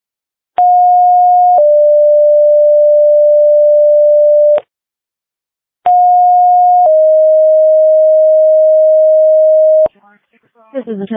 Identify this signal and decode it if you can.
Two-tone page, heard as a one-second tone, then a three-second tone, about 720 then 590 Hz, about 720 then 620 Hz, with speech. It is heard from 0.6 to 9.9 s.